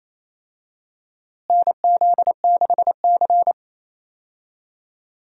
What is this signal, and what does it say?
Morse code, heard as a keyed tone, about 28 wpm, 700 Hz. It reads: NZ6C
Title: USA Call Signs - 4 Characters - Repeated 2x - 28wpm